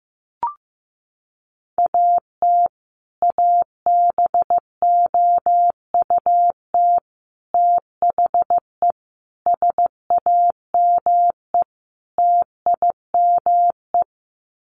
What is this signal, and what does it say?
Morse code, heard as a keyed tone, about 15 wpm, 700 Hz. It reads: AT ABOUT THE SAME TIME
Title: Sets of 5 Words Encore 15wpm